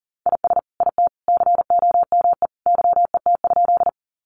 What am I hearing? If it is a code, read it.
Morse code, heard as a keyed tone, about 38 wpm, 700 Hz. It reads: IS IT COME YET?
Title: Sentences from Top 500 Words 40wpm